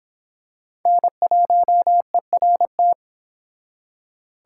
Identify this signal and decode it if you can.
Morse code, heard as a keyed tone, about 25 wpm, 700 Hz. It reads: N1ERT